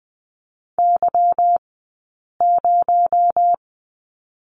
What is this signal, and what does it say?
Morse code, heard as a keyed tone, about 20 wpm, 700 Hz. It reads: Y 0
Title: 217 Lesson 35d - Number 0 - Two Characters 20wpm